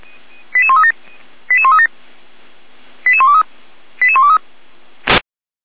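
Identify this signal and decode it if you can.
Five-tone selective call (ZVEI), heard as quick runs of five tones, tones of about 70 ms. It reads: 88137 88137 88123 88123
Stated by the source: Received on an SDR; ZVEI five-tone call.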